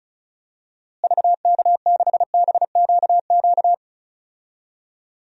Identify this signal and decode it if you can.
Morse code, heard as a keyed tone, about 36 wpm, 700 Hz. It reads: VK6BQQ
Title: DXCC-100 Call Signs - 6 Characters - Repeated 2x - 35wpm